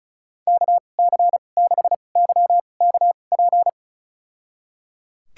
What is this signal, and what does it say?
Morse code, heard as a keyed tone, about 36 wpm, spaced as for 25 wpm, 700 Hz. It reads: KC6YKP
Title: USA Call Signs - 2x Character Spacing - 35wpm